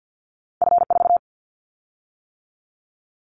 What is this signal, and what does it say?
Morse code, heard as a keyed tone, about 50 wpm, 700 Hz. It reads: F4
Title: Callsign Prefixes for Top 50 DXCC - 50wpm